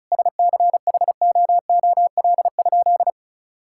Morse code, heard as a keyed tone, about 36 wpm, 700 Hz. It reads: SCHOOL?
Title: Sentences from Top 1000 Words - Review Part 6 - 35wpm